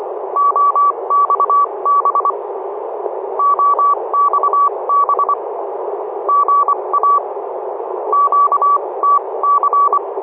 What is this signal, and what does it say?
Morse code, heard as a keyed tone, about 25 wpm, spaced as for 21 wpm, 1140 Hz, with radio noise. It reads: OXB OXB GA QTC
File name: oxb-tfc-list